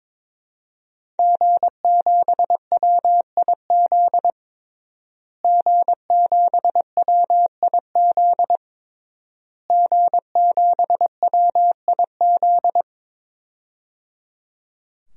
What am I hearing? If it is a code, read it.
Morse code, heard as a keyed tone, about 22 wpm, 700 Hz. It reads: G7WIZ G7WIZ G7WIZ
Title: DXCC-100 Call Signs - 5 Characters - Repeated 3x - 22wpm